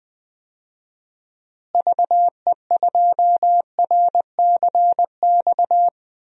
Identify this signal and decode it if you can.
Morse code, heard as a keyed tone, about 21 wpm, 700 Hz. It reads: VE2RCX
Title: DXCC-100 Call Signs - 6 Characters - Repeated 3x - 20wpm